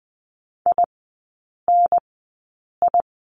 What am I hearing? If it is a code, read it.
Morse code, heard as a keyed tone, about 20 wpm, spaced as for 8 wpm, 700 Hz. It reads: INI
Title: 107 Lesson 18e - Letter m - Three Characters 20wpm